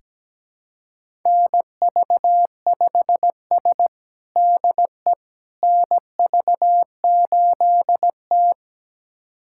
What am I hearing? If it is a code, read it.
Morse code, heard as a keyed tone, about 17 wpm, 700 Hz. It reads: NV5S DE NV8T
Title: USA Callsign DE Callsign - 4 Characters - 17wpm